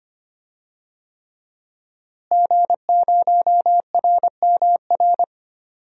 Morse code, heard as a keyed tone, about 25 wpm, 700 Hz. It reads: G0RMR
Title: DXCC-100 Call Signs - 5 Characters - Repeated 3x - 25wpm